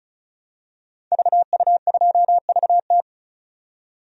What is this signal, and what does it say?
Morse code, heard as a keyed tone, about 36 wpm, 700 Hz. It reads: VU2VT
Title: DXCC-100 Call Signs - 5 Characters - 35wpm